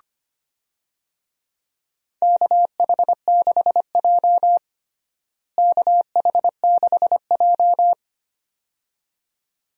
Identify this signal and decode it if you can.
Morse code, heard as a keyed tone, about 25 wpm, 700 Hz. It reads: KH6J KH6J
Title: DXCC-100 Call Signs - 4 Characters - Repeated 2x - 25wpm